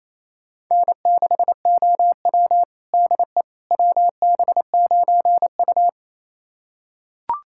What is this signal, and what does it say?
Morse code, heard as a keyed tone, about 28 wpm, 700 Hz. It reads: N6OW DE WB9U
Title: USA Callsign DE Callsign - 4 Characters - 28wpm